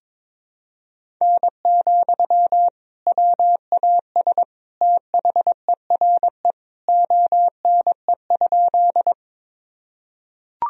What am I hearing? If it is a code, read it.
Morse code, heard as a keyed tone, about 22 wpm, 700 Hz. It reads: N, WAS THERE ONE?
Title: Sentences from Top 100 Words and Abbreviations 22wpm